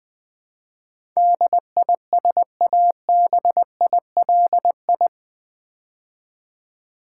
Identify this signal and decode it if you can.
Morse code, heard as a keyed tone, about 20 wpm, 700 Hz. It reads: DISABILI
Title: Build A Word - 4 Letters At A Time - 10-Letter Words - 20wpm